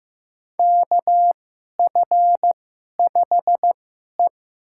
Morse code, heard as a keyed tone, about 15 wpm, spaced as for 11 wpm, 700 Hz. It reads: KF5E
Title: USA Call Signs - 4 Characters - 2x Character Spacing - 15wpm